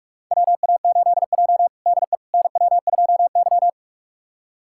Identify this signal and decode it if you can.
Morse code, heard as a keyed tone, about 46 wpm, 700 Hz. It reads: WA8J DE NW2Y
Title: USA Callsign DE Callsign - 4 Characters - 45wpm